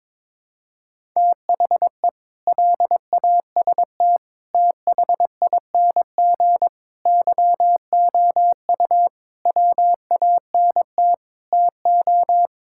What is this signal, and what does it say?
Morse code, heard as a keyed tone, about 22 wpm, 700 Hz. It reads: THE LAST THING YOU WANT TO
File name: practice / 22wpm / Sentences with 8-Word Phrases - 22wpm